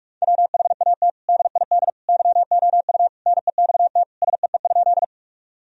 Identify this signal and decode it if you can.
Morse code, heard as a keyed tone, about 46 wpm, 700 Hz. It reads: WHAT DID YOU NEXT SEE?
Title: Sentences from Top 1000 Words - Review Part 1 - 45wpm